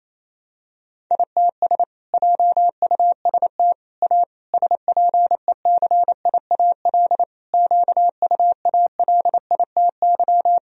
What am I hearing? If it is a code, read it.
Morse code, heard as a keyed tone, about 28 wpm, 700 Hz. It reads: ITS JUST A SPECIAL QUALITY
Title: Sentences from Top 1000 Words - Review Part 4 - 28wpm